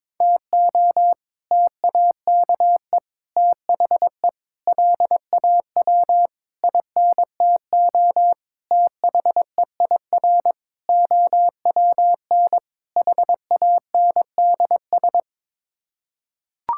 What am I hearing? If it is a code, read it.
Morse code, heard as a keyed tone, about 22 wpm, 700 Hz. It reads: TO TAKE THE LAW INTO THEIR OWN HANDS
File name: practice / 22wpm / Sets of 8 Words - 22wpm